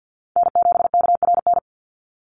Morse code, heard as a keyed tone, about 50 wpm, 700 Hz. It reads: N7KRN
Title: NA SOTA Chaser Callsigns 50wpm